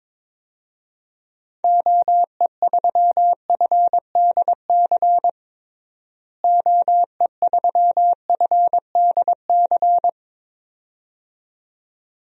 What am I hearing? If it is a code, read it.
Morse code, heard as a keyed tone, about 22 wpm, 700 Hz. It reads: OE3FDC OE3FDC
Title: DXCC-100 Call Signs - 6 Characters - Repeated 2x - 22wpm